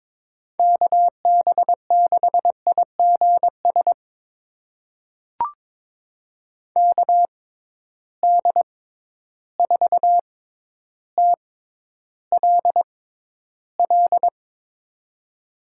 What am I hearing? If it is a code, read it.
Morse code, heard as a keyed tone, about 22 wpm, spaced as for 11 wpm, 700 Hz. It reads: KB6IGS KD4TLL
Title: USA Call Signs - 6 Characters - 6x Character Spacing - 22wpm